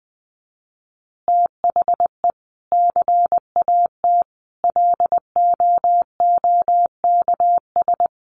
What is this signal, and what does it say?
Morse code, heard as a keyed tone, about 20 wpm, 700 Hz. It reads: THE CAT LOOKS
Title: Additional Easy Phrases 20wpm